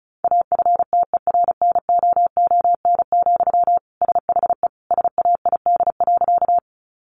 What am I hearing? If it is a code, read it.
Morse code, heard as a keyed tone, about 34 wpm, 700 Hz. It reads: AFTERNOON, SHE SAID.
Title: Sentences from Top 1500 Words 35wpm